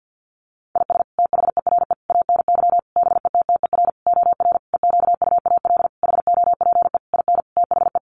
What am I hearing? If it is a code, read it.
Morse code, heard as a keyed tone, about 50 wpm, 700 Hz. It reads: IS THERE ANY BETTER OR EQUAL HOPE IN THE
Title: Sentences from Top 1400 Words 50wpm